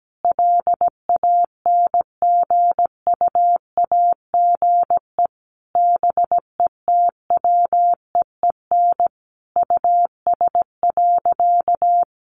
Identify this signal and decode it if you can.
Morse code, heard as a keyed tone, about 17 wpm, 700 Hz. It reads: LANGUAGE BETWEEN US.
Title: Sentences from Top 500 Words 17wpm